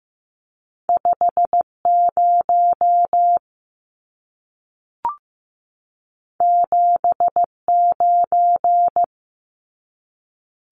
Morse code, heard as a keyed tone, about 15 wpm, 700 Hz. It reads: 5 0 7 9
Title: Numbers - 2-Digit 15wpm